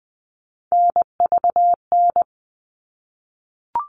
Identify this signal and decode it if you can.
Morse code, heard as a keyed tone, about 20 wpm, 700 Hz. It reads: NVN
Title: All Cut Numbers - 3-Digit 20wpm